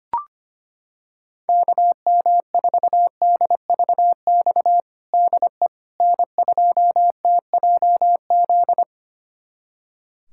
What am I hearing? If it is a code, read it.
Morse code, heard as a keyed tone, about 25 wpm, 700 Hz. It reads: KM4DVX DE N2TJZ
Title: USA Callsign DE Callsign 25wpm